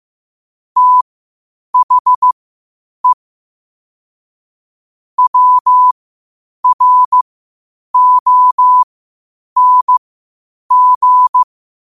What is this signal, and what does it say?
Morse code, heard as a keyed tone, about 15 wpm, spaced as for 8 wpm, 1000 Hz. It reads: THE WRONG
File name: 8wpm